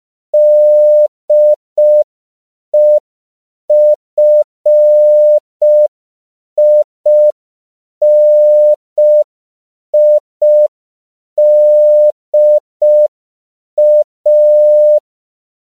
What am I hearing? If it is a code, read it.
Morse code, heard as a keyed tone, about 5 wpm, 600 Hz. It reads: DEFINIDA
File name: CW / 5PPM